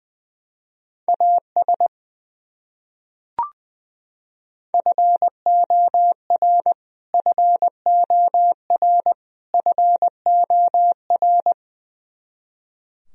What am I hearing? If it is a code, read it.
Morse code, heard as a keyed tone, about 20 wpm, 700 Hz. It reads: AS FOR FOR FOR
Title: Top 100 Words Repeated 20wpm